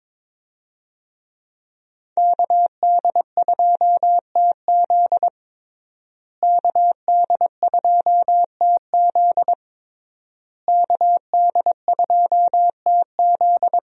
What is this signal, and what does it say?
Morse code, heard as a keyed tone, about 22 wpm, 700 Hz. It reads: KD2TZ KD2TZ KD2TZ
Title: USA Call Signs - 5 Characters - Repeated 3x - 22wpm